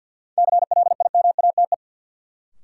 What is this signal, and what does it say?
Morse code, heard as a keyed tone, about 52 wpm, 700 Hz. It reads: CLIMATE
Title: Sentences from Top 1600 Words - New Word Drill Alt - 50wpm